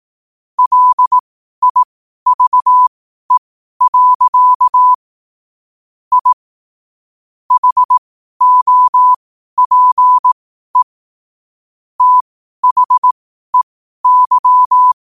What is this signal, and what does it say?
Morse code, heard as a keyed tone, about 18 wpm, spaced as for 12 wpm, 1000 Hz. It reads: LIVE. I HOPE THEY